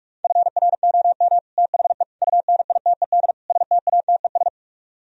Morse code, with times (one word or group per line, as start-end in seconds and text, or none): 0.2-1.4 FROM
1.6-2.0 THE
2.2-3.3 UNITED
3.5-4.5 STATES